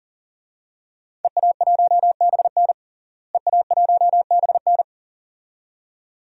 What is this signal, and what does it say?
Morse code, heard as a keyed tone, about 37 wpm, 700 Hz. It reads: EA1BN EA1BN
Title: DXCC-100 Call Signs - 5 Characters - Repeated 2x - 40wpm